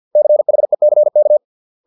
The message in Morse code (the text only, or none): CHECK